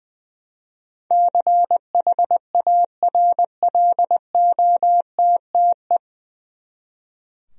Morse code, heard as a keyed tone, about 20 wpm, 700 Hz. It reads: CHARLOTTE
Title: USA Common Names - 20wpm